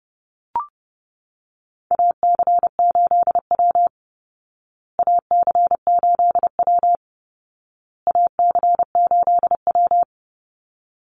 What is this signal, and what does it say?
Morse code, heard as a keyed tone, about 30 wpm, 700 Hz. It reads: AC8W AC8W AC8W
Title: Call Signs Repeated 2x - 30wpm